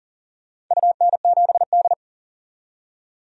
Morse code, heard as a keyed tone, about 38 wpm, 700 Hz. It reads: UN7D
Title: DXCC-100 Call Signs - 4 Characters - 40wpm